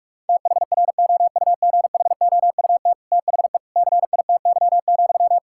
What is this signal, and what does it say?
Morse code, heard as a keyed tone, about 45 wpm, 700 Hz. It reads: THROUGHOUT THE CITY,
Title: Sentences from Top 1000 Words - Review Part 5 - 45wpm